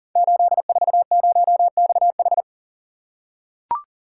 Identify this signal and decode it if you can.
Morse code, heard as a keyed tone, about 39 wpm, 700 Hz. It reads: 840XH
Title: Code Groups - 5 Letter:Numbers - 40wpm